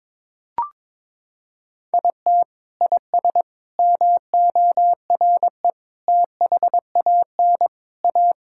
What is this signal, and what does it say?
Morse code, heard as a keyed tone, about 22 wpm, 700 Hz. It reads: IT IS MORE THAN A
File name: practice / 22wpm / Sentences with 3-Word Phrases - 22wpm